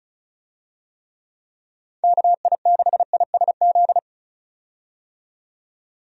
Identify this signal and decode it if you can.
Morse code, heard as a keyed tone, about 36 wpm, 700 Hz. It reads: KI6ISZ
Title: USA Call Signs - 6 Characters - 35wpm